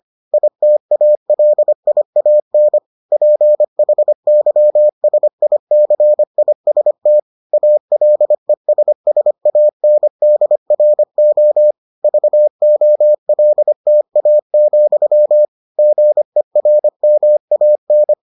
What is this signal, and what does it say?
Morse code, heard as a keyed tone, about 25 wpm, 600 Hz. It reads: ITALIAN PHYSICIST ALESSANDRO VOLTA, GERMAN